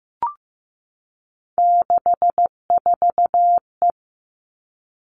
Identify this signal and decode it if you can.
Morse code, heard as a keyed tone, about 15 wpm, 700 Hz. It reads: 64E
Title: All Cut Numbers - 3-Digit 15wpm